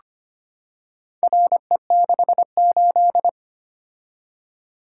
Morse code, heard as a keyed tone, about 25 wpm, 700 Hz. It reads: RE68
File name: practice / 25wpm / Four Letter-Number Combinations - 25wpm